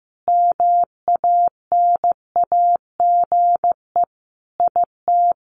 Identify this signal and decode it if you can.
Morse code, heard as a keyed tone, about 15 wpm, 700 Hz. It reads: MANAGE IT
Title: Sentences from Top 1400 Words 15wpm